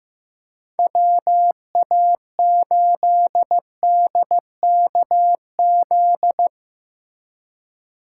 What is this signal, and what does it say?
Morse code, heard as a keyed tone, about 15 wpm, 700 Hz. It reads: WA8DKZ
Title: USA Call Signs - 6 Characters - 2x Character Spacing - 15wpm